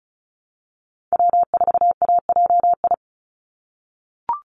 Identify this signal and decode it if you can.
Morse code, heard as a keyed tone, about 34 wpm, 700 Hz. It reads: W4AJI
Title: USA Call Signs 35wpm